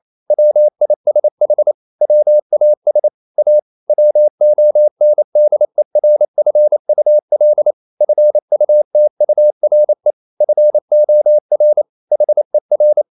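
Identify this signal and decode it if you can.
Morse code, heard as a keyed tone, about 28 wpm, 600 Hz. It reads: WISH WAS A WONDERFUL FUTURE FOR HER
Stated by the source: Morse code practice at 28 wpm, text of Ken Follett's novel Never in English, from CW med Gnister.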